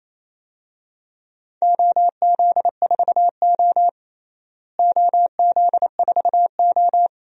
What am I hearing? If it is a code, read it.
Morse code, heard as a keyed tone, about 28 wpm, 700 Hz. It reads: OZ4O OZ4O
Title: DXCC-100 Call Signs - 4 Characters - Repeated 2x - 28wpm